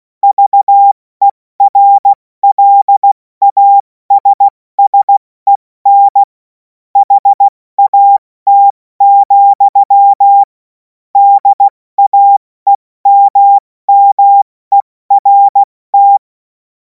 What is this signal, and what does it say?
Morse code, heard as a keyed tone, about 16 wpm, 800 Hz. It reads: VERLASSEN HAT, DAEMMERT
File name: Morsekurs / Witze / Witz 07. April 2026 70 F80